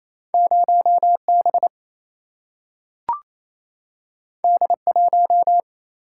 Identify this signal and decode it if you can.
Morse code, heard as a keyed tone, about 28 wpm, 700 Hz. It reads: 0 B D 1
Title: Single-Letter Single-Number 28wpm